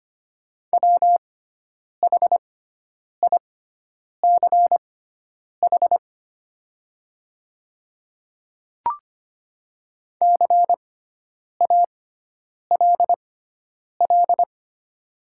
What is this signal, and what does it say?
Morse code, heard as a keyed tone, about 25 wpm, spaced as for 9 wpm, 700 Hz. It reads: WHICH CALL
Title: 6x Character Spacing - Top 100 Words - 25wpm